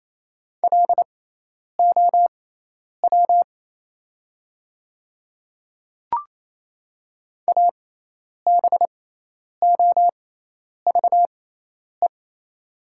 Morse code, heard as a keyed tone, about 28 wpm, spaced as for 10 wpm, 700 Hz. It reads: LOW ABOVE
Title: 6x Character Spacing - Top 500 Words - 28wpm